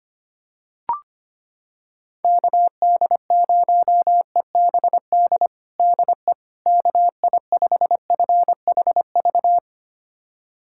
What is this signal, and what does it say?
Morse code, heard as a keyed tone, about 25 wpm, 700 Hz. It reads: KD0EBD DE KI5FHV
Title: USA Callsign DE Callsign - 6 Characters - 25wpm